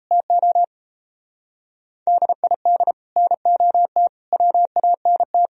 Morse code, heard as a keyed tone, about 33 wpm, 700 Hz. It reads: TO DID NOT WANT
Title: Speed-Racing - Sets of 4 Words 28wpm